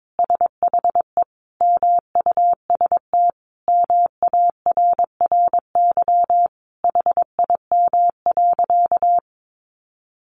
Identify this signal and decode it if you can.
Morse code, heard as a keyed tone, about 22 wpm, 700 Hz. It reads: SHE MUST MARRY HIM.
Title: Sentences from Top 1200 Words 22wpm